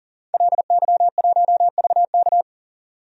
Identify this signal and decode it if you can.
Morse code, heard as a keyed tone, about 40 wpm, 700 Hz. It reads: LY1VK
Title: DXCC-100 Call Signs - 5 Characters - 40wpm